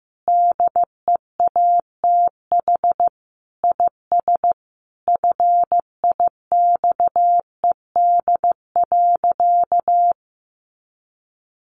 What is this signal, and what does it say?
Morse code, heard as a keyed tone, about 15 wpm, 700 Hz. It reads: DEATH IS FIXED.
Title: Sentences from Top 1100 Words 15wpm